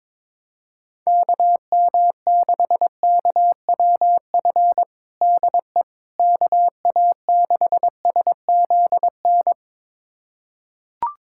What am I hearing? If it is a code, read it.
Morse code, heard as a keyed tone, about 21 wpm, 700 Hz. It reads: KM6KWF DE KA6SZN
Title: USA Callsign DE Callsign - 6 Characters - 22wpm